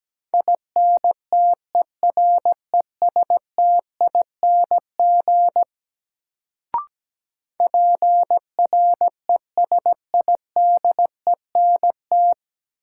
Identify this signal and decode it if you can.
Morse code, heard as a keyed tone, about 17 wpm, 700 Hz. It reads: INTERESTING PRESIDENT
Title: Sentences from Top 1000 Words - New Word Drill - 17wpm